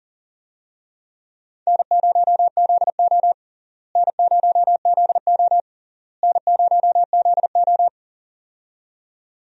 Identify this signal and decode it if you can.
Morse code, heard as a keyed tone, about 40 wpm, 700 Hz. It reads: N0ZO N0ZO N0ZO
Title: USA Call Signs - 4 Characters - Repeated 3x - 40wpm